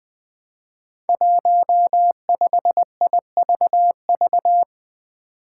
Morse code, heard as a keyed tone, about 20 wpm, 700 Hz. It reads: 15IVV
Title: Five Challenging Letter-Number Combinations 20wpm